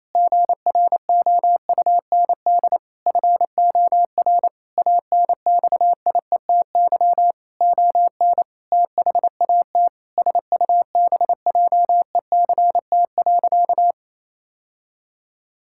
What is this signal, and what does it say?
Morse code, heard as a keyed tone, about 27 wpm, 700 Hz. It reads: GROUND FOR ANXIETY ON THAT SUBJECT.
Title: Sentences from Top 1600 Words 28wpm